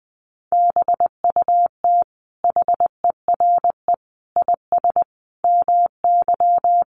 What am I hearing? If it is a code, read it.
Morse code, heard as a keyed tone, about 19 wpm, 700 Hz. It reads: BUT HERE IS MY
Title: Sentences from Top 1500 Words 20wpm